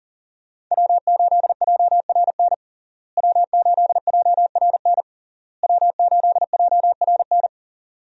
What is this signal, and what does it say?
Morse code, heard as a keyed tone, about 40 wpm, 700 Hz. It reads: W8JRN W8JRN W8JRN
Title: USA Call Signs - 5 Characters - Repeated 3x - 40wpm